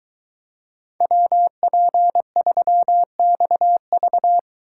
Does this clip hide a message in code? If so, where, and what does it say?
Yes: Morse code from 1.0 to 4.4 s, heard as a keyed tone, about 23 wpm, 700 Hz. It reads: WP3XV